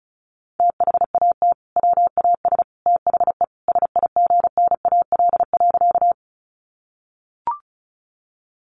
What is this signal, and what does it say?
Morse code, heard as a keyed tone, about 34 wpm, 700 Hz. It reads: THAT WAS THE SIGNAL.
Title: Sentences from Top 1400 Words 35wpm